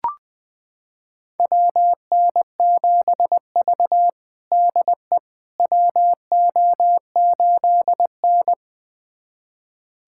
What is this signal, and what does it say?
Morse code, heard as a keyed tone, about 20 wpm, 700 Hz. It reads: WN7V DE WO8N
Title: USA Callsign DE Callsign - 4 Characters - 20wpm